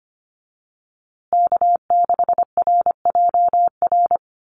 Morse code, heard as a keyed tone, about 25 wpm, 700 Hz. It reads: K6RJR
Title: USA Call Signs Repeated 2x 25wpm